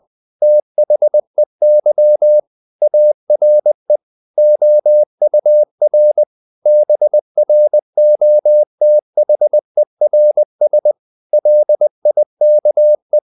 Morse code, heard as a keyed tone, about 20 wpm, 600 Hz. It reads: THEY ARE OUR BROTHERS LIKE
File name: the old man and the sea / the old man and the sea part 1 20wpm